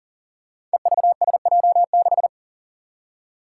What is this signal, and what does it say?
Morse code, heard as a keyed tone, about 42 wpm, 700 Hz. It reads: EVSJ6
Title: Five Challenging Letter-Number Combinations 40wpm